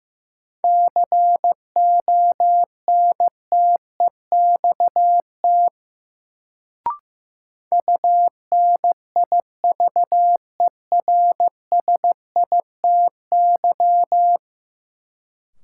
Morse code, heard as a keyed tone, about 15 wpm, 700 Hz. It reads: CONTEXT UNIVERSITY
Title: Sets of 4 Words - New Word Drill - 15wpm